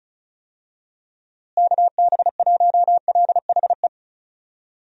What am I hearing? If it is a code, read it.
Morse code, heard as a keyed tone, about 34 wpm, 700 Hz. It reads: KB1LHE
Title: USA Call Signs - 6 Characters - Repeated 2x - 35wpm